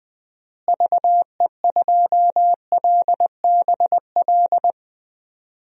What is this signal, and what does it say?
Morse code, heard as a keyed tone, about 20 wpm, 700 Hz. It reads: VE2LBL
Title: DXCC-100 Call Signs - 6 Characters - Repeated 3x - 20wpm